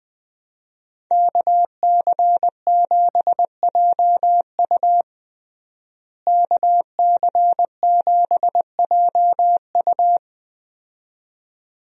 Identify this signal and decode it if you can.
Morse code, heard as a keyed tone, about 20 wpm, 700 Hz. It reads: KC7JU KC7JU
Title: USA Call Signs - 5 Characters - Repeated 2x - 20wpm